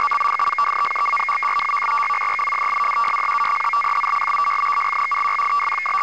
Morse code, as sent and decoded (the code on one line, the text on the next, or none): none